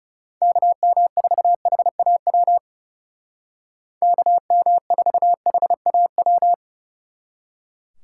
Morse code, heard as a keyed tone, about 39 wpm, 700 Hz. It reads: KM4HAW KM4HAW